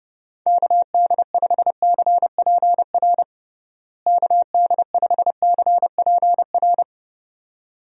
Morse code, heard as a keyed tone, about 30 wpm, 700 Hz. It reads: KD5CPR KD5CPR
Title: USA Call Signs - 6 Characters - Repeated 2x - 30wpm